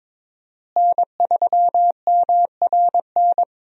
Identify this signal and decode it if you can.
Morse code, heard as a keyed tone, about 21 wpm, 690 Hz. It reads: N3MRN
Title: USA Call Signs - 5 Characters - Repeated 3x - 22wpm